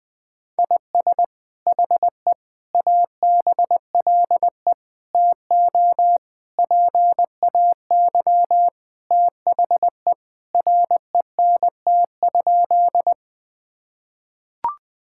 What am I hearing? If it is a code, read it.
Morse code, heard as a keyed tone, about 21 wpm, 700 Hz. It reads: IS HE ABLE TO PAY THE RENT?
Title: Sentences from Top 1000 Words - Review Part 5 - 20wpm